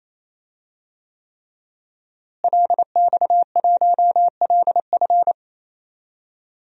Morse code, heard as a keyed tone, about 28 wpm, 700 Hz. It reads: LX1LF
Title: DXCC-100 Call Signs - 5 Characters - Repeated 3x - 28wpm